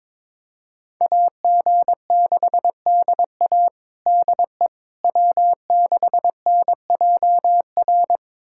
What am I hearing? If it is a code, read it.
Morse code, heard as a keyed tone, about 22 wpm, 700 Hz. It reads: AG6DA DE W6NJR
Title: USA Callsign DE Callsign - 5 Characters - 22wpm